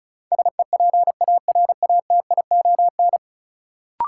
Morse code, heard as a keyed tone, about 36 wpm, 700 Hz. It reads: SEPARATION
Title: Sets of 5 Words - New Word Drill - 35wpm